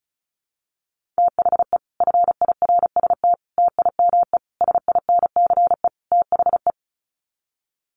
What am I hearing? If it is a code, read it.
Morse code, heard as a keyed tone, about 34 wpm, 700 Hz. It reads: THE FIRST TIME SINCE THE
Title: Sets of 5 Words Encore 35wpm